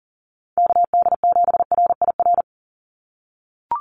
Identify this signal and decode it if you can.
Morse code, heard as a keyed tone, about 40 wpm, 700 Hz. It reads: KD7RIR
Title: USA Call Signs 40wpm